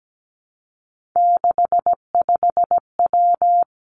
Morse code, heard as a keyed tone, about 17 wpm, 700 Hz. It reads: 65W
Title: Three Challenging Letter-Number Combinations 17wpm